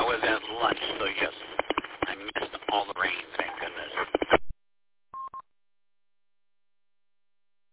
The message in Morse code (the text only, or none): none